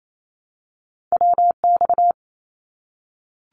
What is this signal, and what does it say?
Morse code, heard as a keyed tone, about 28 wpm, 700 Hz. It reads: WX
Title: Abbreviations 28wpm